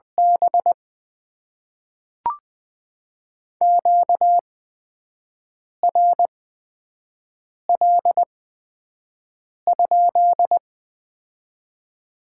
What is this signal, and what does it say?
Morse code, heard as a keyed tone, about 20 wpm, spaced as for 5 wpm, 700 Hz. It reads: B QRL?
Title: Top 90 QSO Elements - 8x Character Spacing - 20wpm